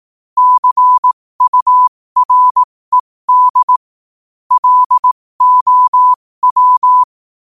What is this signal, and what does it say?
Morse code, heard as a keyed tone, about 18 wpm, spaced as for 15 wpm, 1000 Hz. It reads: CURED LOW
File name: qotdfiles / 15wpm